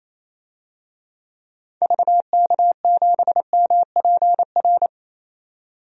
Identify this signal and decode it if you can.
Morse code, heard as a keyed tone, about 28 wpm, 700 Hz. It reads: VK7MPR
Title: DXCC-100 Call Signs - 6 Characters - 28wpm